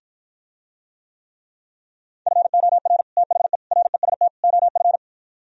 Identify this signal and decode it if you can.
Morse code, heard as a keyed tone, about 52 wpm, 700 Hz. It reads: FOR THE REST OF